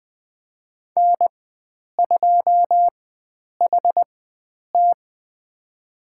Morse code, heard as a keyed tone, about 21 wpm, spaced as for 9 wpm, 700 Hz. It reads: N2HT
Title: USA Call Signs - 4x Character Spacing - 20wpm